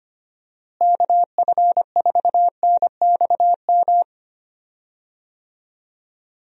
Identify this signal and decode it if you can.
Morse code, heard as a keyed tone, about 25 wpm, 700 Hz. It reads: KF4NXM
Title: USA Call Signs - 6 Characters - Repeated 2x - 25wpm